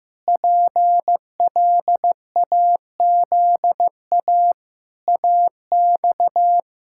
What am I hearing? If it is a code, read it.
Morse code, heard as a keyed tone, about 15 wpm, 700 Hz. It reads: PLAZA AX
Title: Pangram Challenge v2 - 15wpm